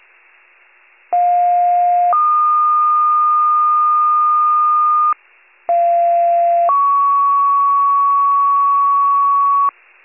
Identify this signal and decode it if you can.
Two-tone page, heard as a one-second tone, then a three-second tone, about 710 then 1220 Hz, about 690 then 1120 Hz. It is heard from 1.1 to 9.7 s.
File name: wcalert